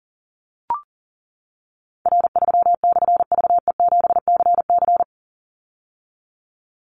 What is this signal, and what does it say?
Morse code, heard as a keyed tone, about 40 wpm, 700 Hz. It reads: R3/VE7CC
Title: Call Signs - 40wpm